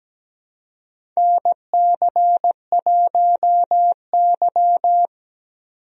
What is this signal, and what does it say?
Morse code, heard as a keyed tone, about 17 wpm, 700 Hz. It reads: NC1Y
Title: USA Call Signs - 4 Characters - 17wpm